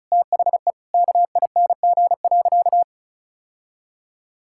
Morse code, heard as a keyed tone, about 36 wpm, 700 Hz. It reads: THE KING.
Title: Sentences from Top 1000 Words - Review Part 6 - 35wpm